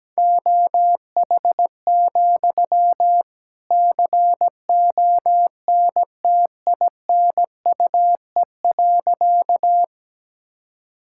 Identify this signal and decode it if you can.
Morse code, heard as a keyed tone, about 17 wpm, 700 Hz. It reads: OH, CONTINUE.
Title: Sentences from Top 1000 Words - Review Part 6 - 17wpm